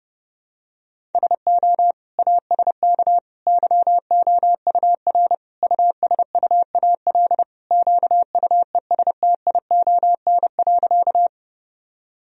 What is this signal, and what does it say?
Morse code, heard as a keyed tone, about 30 wpm, 700 Hz. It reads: SO ASK YOUR USUAL QUESTION.
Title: Sentences from Top 1000 Words - Review Part 2 - 30wpm